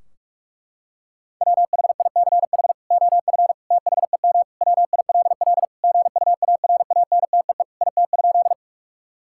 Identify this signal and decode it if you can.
Morse code, heard as a keyed tone, about 45 wpm, 700 Hz. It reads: WHICH OF THEM WILL GUARANTEE IT?
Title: Sentences from Top 1600 Words 45wpm